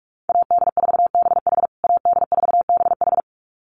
Morse code, heard as a keyed tone, about 44 wpm, 700 Hz. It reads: AD4BH AD4BH
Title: USA Call Signs Repeated 2x 45wpm